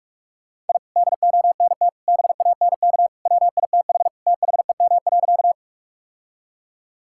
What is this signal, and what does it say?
Morse code, heard as a keyed tone, about 46 wpm, 700 Hz. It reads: I DONT BANK WITH THEM.
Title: Sentences from Top 1000 Words - Review Part 6 - 45wpm